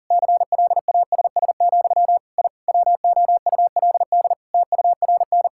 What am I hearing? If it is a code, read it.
Morse code, heard as a keyed tone, about 40 wpm, 700 Hz. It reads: CLASS, I WOULD TURN
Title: Sentences from Top 1000 Words - Review Part 6 - 40wpm